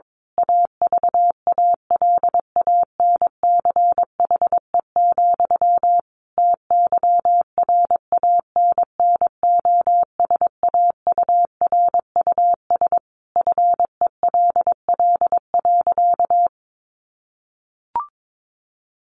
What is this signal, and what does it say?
Morse code, heard as a keyed tone, about 22 wpm, 700 Hz. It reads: AVALANCHE, TYRANNOSAURUS FELL.
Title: A Sound Of Thunder 22wpm